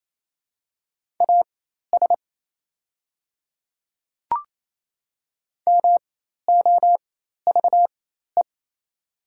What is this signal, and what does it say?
Morse code, heard as a keyed tone, about 28 wpm, spaced as for 13 wpm, 700 Hz. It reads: AS MOVE
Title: 4x Character Spacing - Top 200 Words - 28wpm